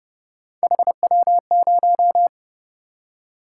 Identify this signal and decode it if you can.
Morse code, heard as a keyed tone, about 28 wpm, 700 Hz. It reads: HW0